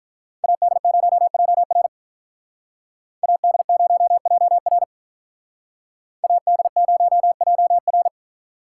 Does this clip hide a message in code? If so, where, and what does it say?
Yes: Morse code from 0.4 to 8.1 s, heard as a keyed tone, about 44 wpm, 700 Hz. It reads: AD0JR AD0JR AD0JR